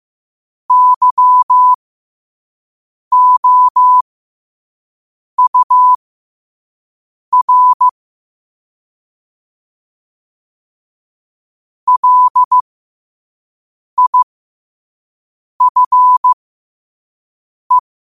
Morse code, heard as a keyed tone, about 15 wpm, spaced as for 5 wpm, 1000 Hz. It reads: YOUR LIFE
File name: qotdfiles / 5wpm